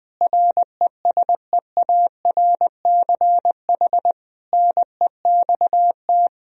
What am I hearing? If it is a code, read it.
Morse code, heard as a keyed tone, about 20 wpm, 700 Hz. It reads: RESEARCH NEXT
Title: Sentences from Top 1000 Words - Review Part 4 - 20wpm